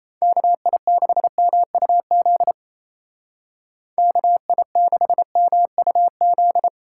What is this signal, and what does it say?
Morse code, heard as a keyed tone, about 33 wpm, 700 Hz. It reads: KI6MUZ KI6MUZ
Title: Speed Racing - USA Call Signs 28wpm